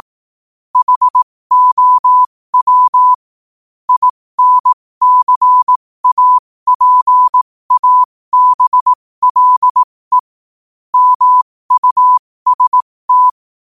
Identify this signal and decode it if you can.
Morse code, heard as a keyed tone, about 18 wpm, spaced as for 15 wpm, 1000 Hz. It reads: HOW INCAPABLE MUST
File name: qotdfiles / 15wpm